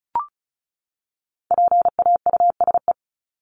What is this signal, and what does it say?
Morse code, heard as a keyed tone, about 34 wpm, 700 Hz. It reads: PAUSE
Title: Top 1501-1750 Words 35wpm